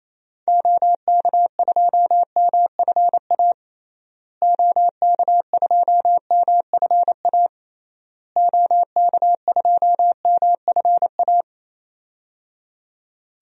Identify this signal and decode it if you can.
Morse code, heard as a keyed tone, about 28 wpm, 700 Hz. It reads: OK2MFA OK2MFA OK2MFA
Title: DXCC-100 Call Signs - 6 Characters - Repeated 3x - 28wpm